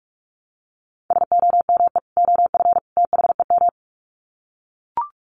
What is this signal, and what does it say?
Morse code, heard as a keyed tone, about 44 wpm, 700 Hz. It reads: SOME OF THEM
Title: Sets of 3 Words 45wpm